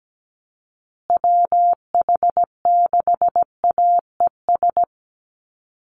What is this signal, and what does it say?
Morse code, heard as a keyed tone, about 17 wpm, 700 Hz. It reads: WH6AES
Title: USA Call Signs Repeated 3x 17wpm